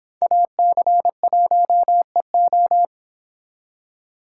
Morse code, heard as a keyed tone, about 26 wpm, 700 Hz. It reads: AC1EO